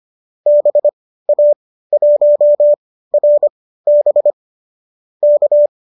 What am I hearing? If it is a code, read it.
Morse code, heard as a keyed tone, about 25 wpm, spaced as for 15 wpm, 590 Hz. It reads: BA1RB K